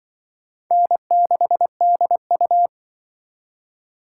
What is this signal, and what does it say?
Morse code, heard as a keyed tone, about 25 wpm, 700 Hz. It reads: N6DU
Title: Speed Racing - USA 1x2 Call Signs 20wpm